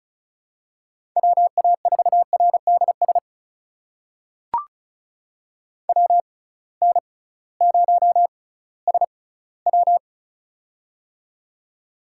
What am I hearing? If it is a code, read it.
Morse code, heard as a keyed tone, about 36 wpm, spaced as for 19 wpm, 700 Hz. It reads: WA4RDS WN0SW
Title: USA Call Signs - 6x Character Spacing - 35wpm